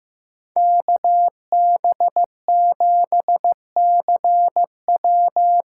Morse code, heard as a keyed tone, about 15 wpm, 700 Hz. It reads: KB7CW
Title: USA Call Signs - 6 Characters - 8x Character Spacing - 15wpm